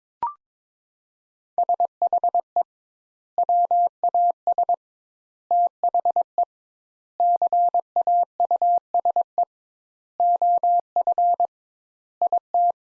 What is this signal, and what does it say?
Morse code, heard as a keyed tone, about 22 wpm, 700 Hz. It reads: SHE WAS THE CAUSE OF IT
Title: Sentences Top 200 Words - 22wpm